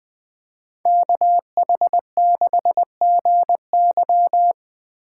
Morse code, heard as a keyed tone, about 24 wpm, 700 Hz. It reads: KH6GY
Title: Speed Racing - USA 2x2 Call Signs 20wpm